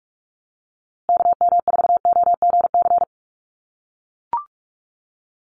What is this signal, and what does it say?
Morse code, heard as a keyed tone, about 44 wpm, 700 Hz. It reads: KM4OGC
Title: USA Call Signs Repeated 3x 45wpm